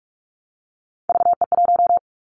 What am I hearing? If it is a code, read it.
Morse code, heard as a keyed tone, about 44 wpm, 700 Hz. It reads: VE1
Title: Callsign Prefixes for Top 100 DXCC - 45wpm